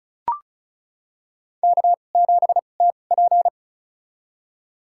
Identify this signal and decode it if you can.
Morse code, heard as a keyed tone, about 36 wpm, spaced as for 25 wpm, 700 Hz. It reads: K7TP
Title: USA Call Signs - 4 Characters - 2x Character Spacing - 35wpm